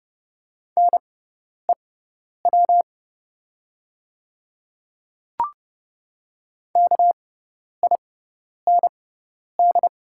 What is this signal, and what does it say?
Morse code, heard as a keyed tone, about 30 wpm, spaced as for 10 wpm, 700 Hz. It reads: NEW KIND
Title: 6x Character Spacing - Top 200 Words - 30wpm